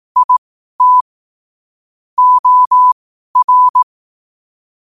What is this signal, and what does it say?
Morse code, heard as a keyed tone, about 18 wpm, spaced as for 12 wpm, 1000 Hz. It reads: IT OR
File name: qotdfiles / 12wpm